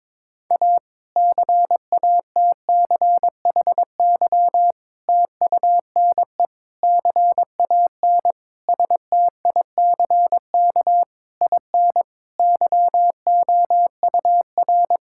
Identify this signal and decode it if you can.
Morse code, heard as a keyed tone, about 22 wpm, 700 Hz. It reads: A CATCHY TUNE CAN STICK IN YOUR
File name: Difficult Sentences with Hints - 22wpm